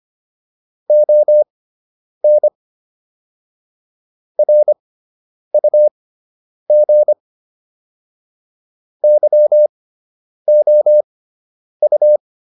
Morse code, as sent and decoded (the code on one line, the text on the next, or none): --- -. / .-. ..- --. / -.-- --- ..-
ON RUG YOU